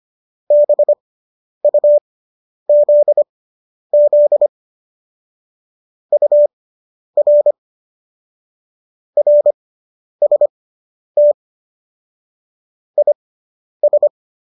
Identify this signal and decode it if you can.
Morse code, heard as a keyed tone, about 25 wpm, spaced as for 10 wpm, 590 Hz. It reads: BUZZ UR RST IS